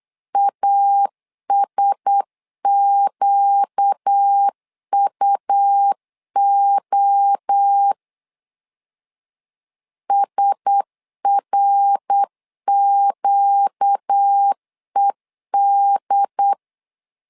Morse code, heard as a keyed tone, about 8 wpm, 800 Hz. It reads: ASQUO SRQED
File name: LEZIONI / 4 Q1S 8wpm 2710